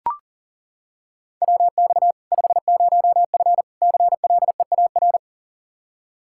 Morse code, heard as a keyed tone, about 40 wpm, 700 Hz. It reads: WX 50F CLEAR
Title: K1USN SST QSO Practice 40wpm